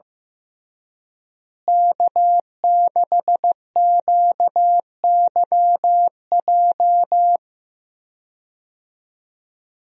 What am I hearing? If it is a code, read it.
Morse code, heard as a keyed tone, about 15 wpm, 700 Hz. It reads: K6QYJ
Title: USA Call Signs - 5 Characters - Repeated 3x - 15wpm